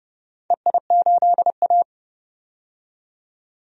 Morse code, heard as a keyed tone, about 31 wpm, 700 Hz. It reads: EI8A
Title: DXCC-100 Call Signs - 4 Characters - 30wpm